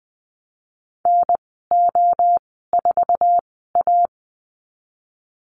Morse code, heard as a keyed tone, about 20 wpm, spaced as for 15 wpm, 700 Hz. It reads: NO4A
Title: 069 Lesson 12g - Letter d - Call Signs 20wpm